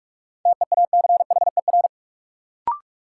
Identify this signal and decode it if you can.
Morse code, heard as a keyed tone, about 44 wpm, 700 Hz. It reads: TEACHER
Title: Sentences from Top 900 Words - New Word Drill - 45wpm